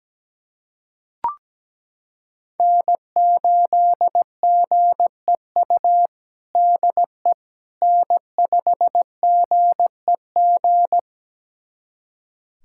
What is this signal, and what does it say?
Morse code, heard as a keyed tone, about 17 wpm, 700 Hz. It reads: N8GEU DE N5GEG
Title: USA Callsign DE Callsign - 5 Characters - 17wpm